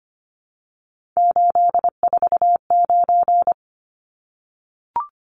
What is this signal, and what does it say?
Morse code, heard as a keyed tone, about 25 wpm, 700 Hz. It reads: 849
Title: Numbers - 3-Digit 25wpm